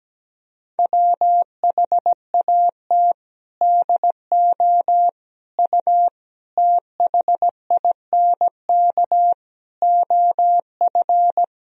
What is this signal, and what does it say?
Morse code, heard as a keyed tone, about 17 wpm, 700 Hz. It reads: WHAT DO U THINK OF
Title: Sentences from Top 200 Words and Abbreviations 17wpm